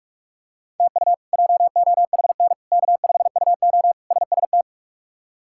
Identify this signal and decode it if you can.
Morse code, heard as a keyed tone, about 46 wpm, 700 Hz. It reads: TU JOHN K5UO SST
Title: K1USN SST QSO Practice - 45wpm